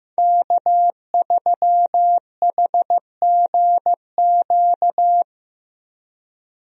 Morse code, heard as a keyed tone, about 15 wpm, 700 Hz. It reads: K3HGQ
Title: USA Call Signs - 5 Characters - Repeated 3x - 15wpm